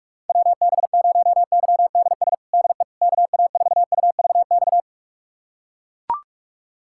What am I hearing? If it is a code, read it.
Morse code, heard as a keyed tone, about 41 wpm, 700 Hz. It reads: WB0YDS DE KA4UVX
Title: USA Callsign DE Callsign - 6 Characters - 45wpm